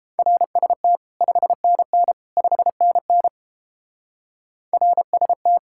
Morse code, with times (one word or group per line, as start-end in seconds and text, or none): 0.2-1.0 RST
1.2-2.1 5NN
2.4-3.3 5NN
4.7-5.6 RST